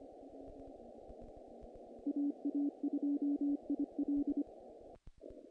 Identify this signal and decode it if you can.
Morse code, heard as a keyed tone, about 23 wpm, 290 Hz. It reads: AA2IL
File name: contests / naqp-cw-k5tr-op-k5pi- -1800